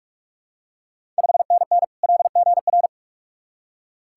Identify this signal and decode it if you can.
Morse code, heard as a keyed tone, about 44 wpm, 690 Hz. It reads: 5NN LGR
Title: Ontario QSO Party Exchange 45wpm